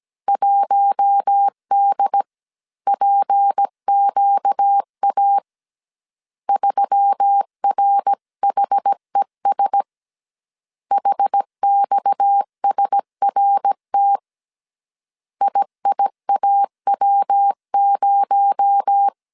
Morse code, heard as a keyed tone, about 17 wpm, 800 Hz. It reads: 1DPQA 3RHES HXSRT IIAW0
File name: CW / DET / 1 6 1500C S A 16wpm 2635